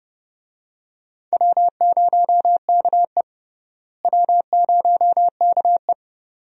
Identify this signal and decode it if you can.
Morse code, heard as a keyed tone, about 31 wpm, 700 Hz. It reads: W0KE W0KE
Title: USA Call Signs - 4 Characters - Repeated 2x - 30wpm